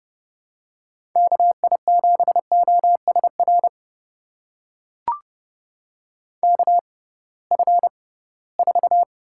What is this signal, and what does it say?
Morse code, heard as a keyed tone, about 32 wpm, spaced as for 20 wpm, 700 Hz. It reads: KI7OSR KF4
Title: USA Call Signs - 6 Characters - 6x Character Spacing - 30wpm